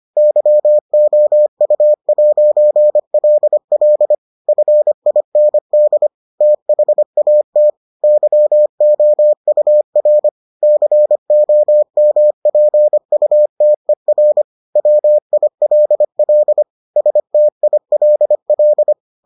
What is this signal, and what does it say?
Morse code, heard as a keyed tone, about 25 wpm, 600 Hz. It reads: YOU'LL FIND THAT YOUR COMPUTER WILL STILL